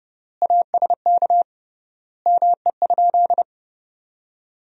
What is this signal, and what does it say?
Morse code, heard as a keyed tone, about 32 wpm, 700 Hz. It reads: ASK ME?
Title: 3x Word-Spacing - Sentences from Top 200 Words - 30wpm